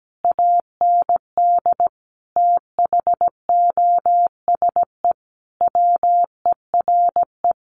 Morse code, heard as a keyed tone, about 17 wpm, 700 Hz. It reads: AND THOSE WERE
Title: A Sound Of Thunder 17wpm